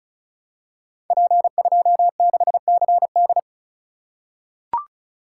Morse code, heard as a keyed tone, about 36 wpm, 690 Hz. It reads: P26CD
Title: Code Groups - 5 Letter:Numbers - 35wpm